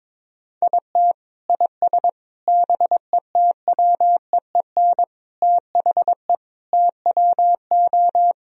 Morse code, heard as a keyed tone, about 21 wpm, 700 Hz. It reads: IT IS BETWEEN THE TWO
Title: Sentences from Top 1000 Words - Review Part 1 - 22wpm